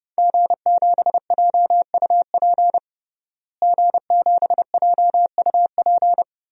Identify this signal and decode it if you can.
Morse code, heard as a keyed tone, about 32 wpm, 700 Hz. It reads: G7JUP G7JUP
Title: DXCC-100 Call Signs - 5 Characters - Repeated 2x - 30wpm